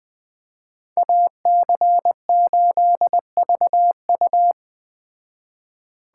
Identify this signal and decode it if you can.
Morse code, heard as a keyed tone, about 20 wpm, 700 Hz. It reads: AC8VU